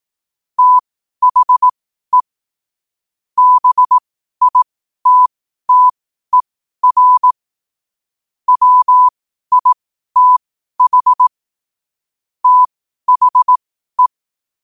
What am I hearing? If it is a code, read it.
Morse code, heard as a keyed tone, about 18 wpm, spaced as for 12 wpm, 1000 Hz. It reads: THE BITTER WITH THE
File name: qotdfiles / 12wpm